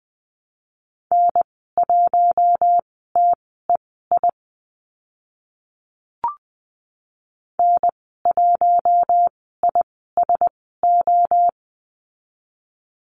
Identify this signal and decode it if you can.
Morse code, heard as a keyed tone, about 20 wpm, spaced as for 14 wpm, 700 Hz. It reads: N1TEI N1ISO
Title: 042 Lesson 8g - Number 1 - Call Signs 20wpm